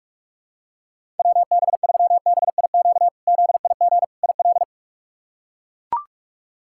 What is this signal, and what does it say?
Morse code, heard as a keyed tone, about 41 wpm, 700 Hz. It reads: WB3BIQ ZIG IL
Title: K1USN SST QSO Practice - 45wpm